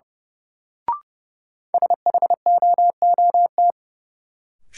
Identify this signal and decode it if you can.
Morse code, heard as a keyed tone, about 31 wpm, 700 Hz. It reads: SHOOT
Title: Sentences from Top 1200 Words - New Word Drill Alt - 30wpm